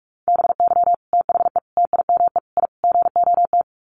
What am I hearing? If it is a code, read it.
Morse code, heard as a keyed tone, about 44 wpm, 700 Hz. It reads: BY THE TIME I GOT
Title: Sets of 5 Words Encore2 45wpm